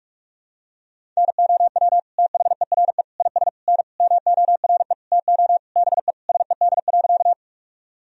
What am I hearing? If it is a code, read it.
Morse code, heard as a keyed tone, about 46 wpm, 700 Hz. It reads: NOW THERE IS N MORE TO BE SED.
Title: Sentences from Top 100 Words and Abbreviations 45wpm